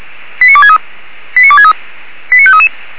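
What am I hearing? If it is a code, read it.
Five-tone selective call (ZVEI), heard as quick runs of five tones, tones of about 70 ms. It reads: 89263 89263 89533
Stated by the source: ZVEI five-tone selective call 89360 received by SDR.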